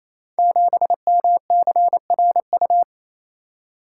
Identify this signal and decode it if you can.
Morse code, heard as a keyed tone, about 27 wpm, 700 Hz. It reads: 7MCRU
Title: Code Groups - 5 Letter:Numbers - 28wpm